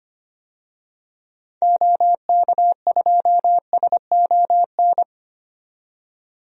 Morse code, heard as a keyed tone, about 25 wpm, 700 Hz. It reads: OK2SON
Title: DXCC-100 Call Signs - 6 Characters - 25wpm